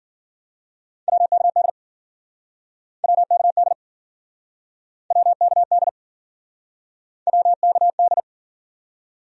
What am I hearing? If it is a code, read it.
Morse code, heard as a keyed tone, about 52 wpm, 700 Hz. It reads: WKD WKD WKD WKD